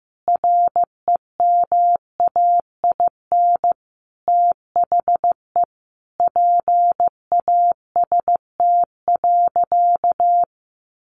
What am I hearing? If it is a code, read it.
Morse code, heard as a keyed tone, about 15 wpm, 700 Hz. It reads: REMAIN THE PAST.
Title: Sentences from Top 1200 Words 15wpm